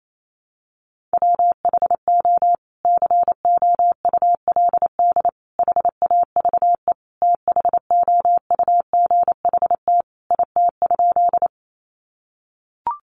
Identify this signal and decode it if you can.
Morse code, heard as a keyed tone, about 28 wpm, 700 Hz. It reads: WHO COULD HAVE THOUGHT IT?
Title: Sentences from Top 300 Words 28wpm